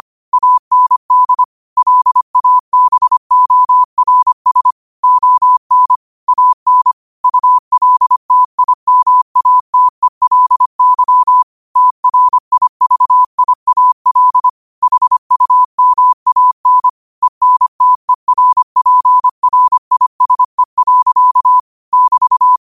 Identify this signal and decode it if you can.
Morse code, heard as a keyed tone, about 25 wpm, 1000 Hz. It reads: AND LABORS ON AN ULTIMATELY TRIVIAL HUMAN ENTERPRISE. <BT>
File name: qotdfiles / 25wpm